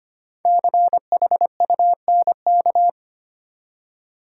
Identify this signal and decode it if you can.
Morse code, heard as a keyed tone, about 25 wpm, 700 Hz. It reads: CHUNK
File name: 5 Letter Words - First 3 Letters Given - 25wpm